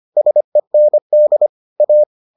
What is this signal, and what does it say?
Morse code, heard as a keyed tone, about 25 wpm, 600 Hz. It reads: SEND A